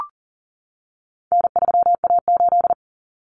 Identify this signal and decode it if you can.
Morse code, heard as a keyed tone, about 40 wpm, 700 Hz. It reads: N3A8
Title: Common Cut Numbers - 4-Digit 40wpm